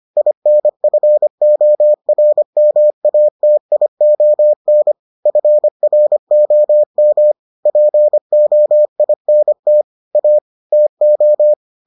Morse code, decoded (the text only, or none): INFORMATION FROM POINT A TO